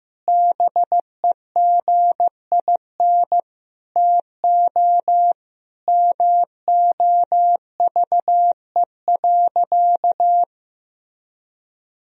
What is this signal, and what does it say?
Morse code, heard as a keyed tone, about 15 wpm, 700 Hz. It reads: BEGIN TO MOVE.
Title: Sentences from Top 1000 Words - Review Part 2 - 15wpm